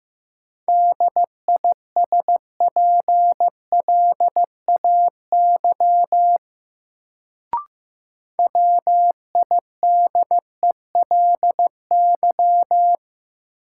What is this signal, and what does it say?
Morse code, heard as a keyed tone, about 15 wpm, 700 Hz. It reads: DISPLAY WIDELY
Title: Sentences from Top 1200 Words - New Word Drill Alt - 15wpm